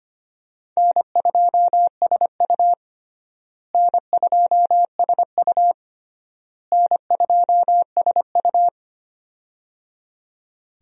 Morse code, heard as a keyed tone, about 25 wpm, 700 Hz. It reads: N2SU N2SU N2SU
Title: USA Call Signs - 4 Characters - Repeated 3x - 25wpm